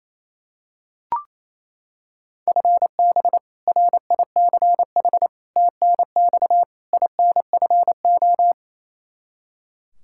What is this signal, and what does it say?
Morse code, heard as a keyed tone, about 28 wpm, 700 Hz. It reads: FB RICH TNX INFO
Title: Practice QSO - Exchange 3 - More Info - 28wpm